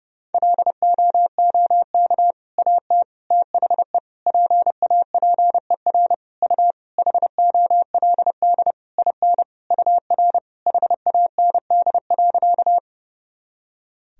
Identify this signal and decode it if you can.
Morse code, heard as a keyed tone, about 30 wpm, 700 Hz. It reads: LOOK AT THE PAPER U HOLD IN UR HAND.
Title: Sentences from Top 400 Words and Abbreviations 30wpm